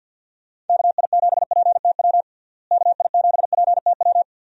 Morse code, heard as a keyed tone, about 48 wpm, 700 Hz. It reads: KI7PTW KI7PTW
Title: USA Call Signs - 6 Characters - Repeated 2x - 50wpm